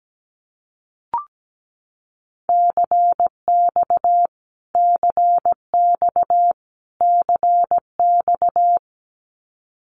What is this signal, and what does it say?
Morse code, heard as a keyed tone, about 17 wpm, 700 Hz. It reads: CX CX CX
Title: Abbreviations Repeated 3x - 17wpm